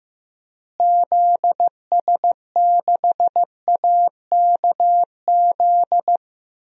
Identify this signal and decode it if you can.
Morse code, heard as a keyed tone, about 15 wpm, 700 Hz. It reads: ZS6AKZ
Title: DXCC-100 Call Signs - 6 Characters - Repeated 3x - 15wpm